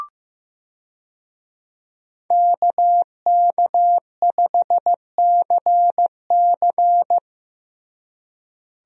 Morse code, heard as a keyed tone, about 15 wpm, 700 Hz. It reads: KK5CC
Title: USA Call Signs - 5 Characters - Repeated 3x - 15wpm